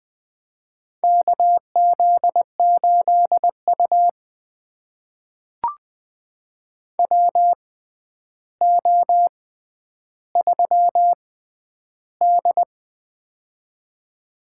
Morse code, heard as a keyed tone, about 20 wpm, spaced as for 10 wpm, 700 Hz. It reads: KZ8U WO3D
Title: USA Call Signs - 4 Characters - 6x Character Spacing - 20wpm